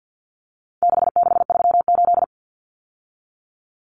Morse code, heard as a keyed tone, about 50 wpm, 700 Hz. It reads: 6638
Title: Numbers - 4-Digit 50wpm